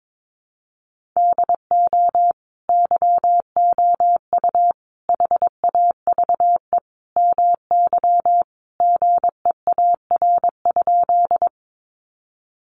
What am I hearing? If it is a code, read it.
Morse code, heard as a keyed tone, about 22 wpm, 700 Hz. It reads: DO YOU HAVE MY GEAR?
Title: Sentences from Top 1400 Words 22wpm